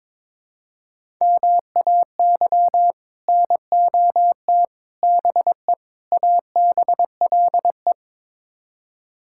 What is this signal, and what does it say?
Morse code, heard as a keyed tone, about 22 wpm, 700 Hz. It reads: MAY NOT BE ABLE
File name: practice / 22wpm / Build A Phrase - 5-Word Phrases - 22wpm